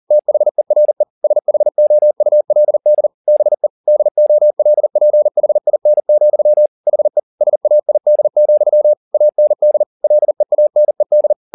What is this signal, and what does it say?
Morse code, heard as a keyed tone, about 40 wpm, 600 Hz. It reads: THERE SHOULD BE DOLPHIN, HE SAID, AND LEANED
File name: the old man and the sea 2 40wpm